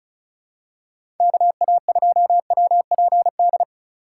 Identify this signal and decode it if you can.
Morse code, heard as a keyed tone, about 36 wpm, 700 Hz. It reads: KA2WPD
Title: USA Call Signs - 6 Characters - 8x Character Spacing - 35wpm